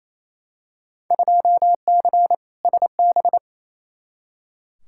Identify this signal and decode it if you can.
Morse code, heard as a keyed tone, about 28 wpm, 700 Hz. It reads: 2C SB
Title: Field Day Exchanges Encore - 28wpm